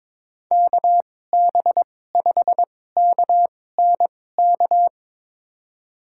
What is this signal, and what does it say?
Morse code, heard as a keyed tone, about 22 wpm, spaced as for 16 wpm, 700 Hz. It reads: KB5KNK
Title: USA Call Signs - 6 Characters - 2x Character Spacing - 22wpm